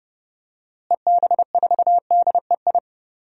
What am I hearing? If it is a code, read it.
Morse code, heard as a keyed tone, about 30 wpm, 700 Hz. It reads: EB4DEI
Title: DXCC-100 Call Signs - 6 Characters - Repeated 3x - 30wpm